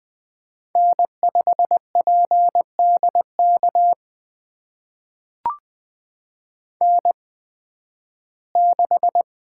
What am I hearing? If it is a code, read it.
Morse code, heard as a keyed tone, about 20 wpm, spaced as for 13 wpm, 700 Hz. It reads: N5PDK N6
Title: USA Call Signs - 8x Character Spacing - 20wpm